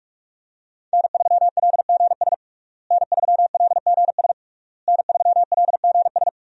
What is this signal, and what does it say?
Morse code, heard as a keyed tone, about 46 wpm, 700 Hz. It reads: N3LGS N3LGS N3LGS
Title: USA Call Signs - 5 Characters - Repeated 3x - 45wpm